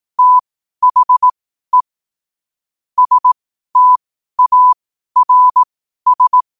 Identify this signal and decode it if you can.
Morse code, heard as a keyed tone, about 18 wpm, spaced as for 12 wpm, 1000 Hz. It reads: THE STARS
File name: qotdfiles / 12wpm